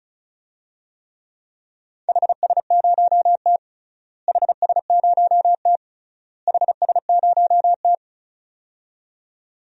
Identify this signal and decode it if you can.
Morse code, heard as a keyed tone, about 36 wpm, 700 Hz. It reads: HS0T HS0T HS0T
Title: DXCC-100 Call Signs - 4 Characters - Repeated 3x - 35wpm